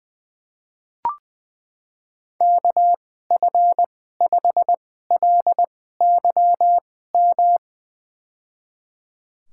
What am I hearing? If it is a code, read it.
Morse code, heard as a keyed tone, about 20 wpm, spaced as for 14 wpm, 700 Hz. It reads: KF5LYM
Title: USA Call Signs - 2x Character Spacing - 20wpm